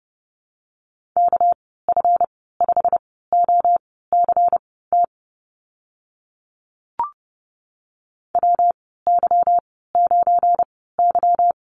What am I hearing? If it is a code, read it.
Morse code, heard as a keyed tone, about 32 wpm, spaced as for 17 wpm, 700 Hz. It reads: KF5OCT WY9Y
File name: Lesson 37g - Letter x - Call Signs 30wpm